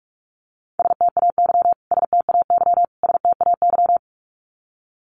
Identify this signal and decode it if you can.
Morse code, heard as a keyed tone, about 44 wpm, 700 Hz. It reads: STAY STAY STAY
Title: Top 500 Words Repeated 45wpm